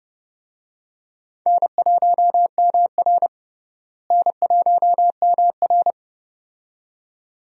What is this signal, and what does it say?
Morse code, heard as a keyed tone, about 30 wpm, 700 Hz. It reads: N1MR N1MR
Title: USA Call Signs - 4 Characters - Repeated 2x - 30wpm